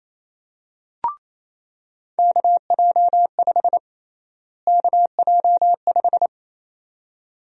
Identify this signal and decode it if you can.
Morse code, heard as a keyed tone, about 28 wpm, 700 Hz. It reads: KJ5 KJ5
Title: USA Call Sign Prefixes - Repeated 2x - 28wpm